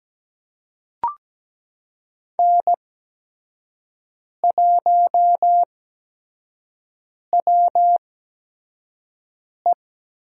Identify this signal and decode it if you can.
Morse code, heard as a keyed tone, about 17 wpm, spaced as for 5 wpm, 700 Hz. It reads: N1WE
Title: USA Call Signs - 5 Characters - 8x Character Spacing - 17wpm